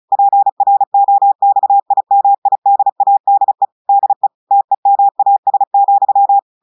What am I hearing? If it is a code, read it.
Morse code, heard as a keyed tone, about 36 wpm, 800 Hz. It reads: PROXIMIDADE DE TEMAS,
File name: audios / portugais / portugais texte en clair 1 vitesse 35 mots minutes